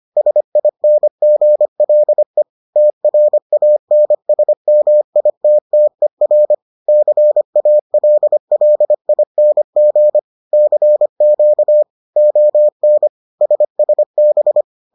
Morse code, heard as a keyed tone, about 25 wpm, 600 Hz. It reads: SINGLE TRANSMITTER CALLING CQ ON SSB